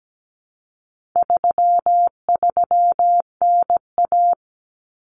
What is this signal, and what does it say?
Morse code, heard as a keyed tone, about 17 wpm, 700 Hz. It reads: 33NA
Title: Common Cut Numbers - 4-Digit 17wpm